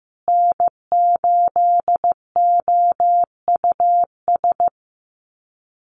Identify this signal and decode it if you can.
Morse code, heard as a keyed tone, about 15 wpm, 700 Hz. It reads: N8OUS
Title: USA Call Signs Repeated 3x 15wpm